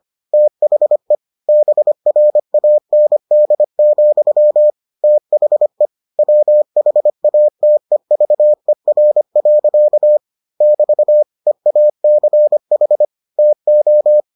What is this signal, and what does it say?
Morse code, heard as a keyed tone, about 25 wpm, 600 Hz. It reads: THE BRAND, THE WHATEVER. <BT> EACH TO